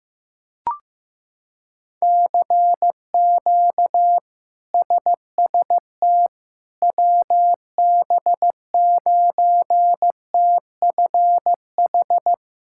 Morse code, heard as a keyed tone, about 15 wpm, 700 Hz. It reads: CQ SST WB9TFH
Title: K1USN SST QSO Practice - 15wpm